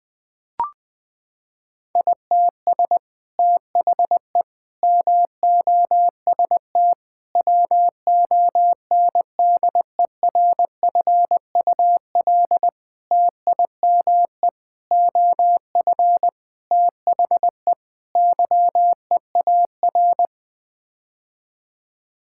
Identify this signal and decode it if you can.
Morse code, heard as a keyed tone, about 20 wpm, 700 Hz. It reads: ITS THE MOST WONDERFUL TIME OF THE YEAR
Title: Sets of 8 Words - 20wpm